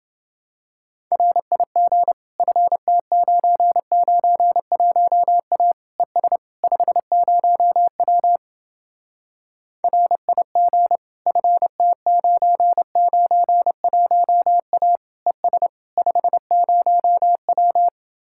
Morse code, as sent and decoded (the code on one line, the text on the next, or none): .-. .. --. / ..-. - ----. ----. .---- .- / . ... / ..... ----- .-- / .-. .. --. / ..-. - ----. ----. .---- .- / . ... / ..... ----- .--
RIG FT991A ES 50W RIG FT991A ES 50W